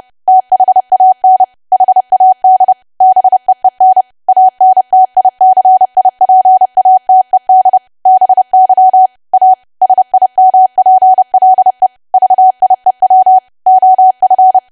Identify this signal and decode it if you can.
Morse code, heard as a keyed tone, about 30 wpm, 750 Hz. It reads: THAN HAD BEEN ANTICIPATED BY A SIMPLE VIEW OF
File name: cwa / W1AW-Level3 / 30WPM